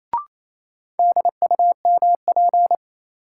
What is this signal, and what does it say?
Morse code, heard as a keyed tone, about 28 wpm, 700 Hz. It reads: DUMP
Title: Sentences from Top 1300 Words - New Word Drill - 28wpm